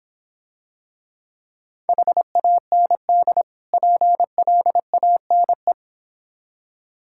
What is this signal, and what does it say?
Morse code, heard as a keyed tone, about 26 wpm, 700 Hz. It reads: HAND PLANE